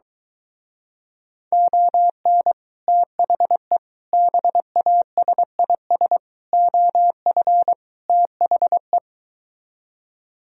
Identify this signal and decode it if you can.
Morse code, heard as a keyed tone, about 23 wpm, 700 Hz. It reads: ON THE BASIS OF THE